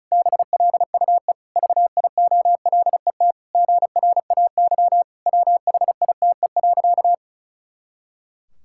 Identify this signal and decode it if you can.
Morse code, heard as a keyed tone, about 36 wpm, 700 Hz. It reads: BLUE VIOLET GRAY WHITE.
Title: Pangram Challenge v2 - 35wpm